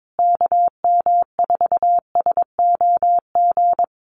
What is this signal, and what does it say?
Morse code, heard as a keyed tone, about 22 wpm, 700 Hz. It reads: KM4SOG
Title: USA Call Signs Repeated 3x 22wpm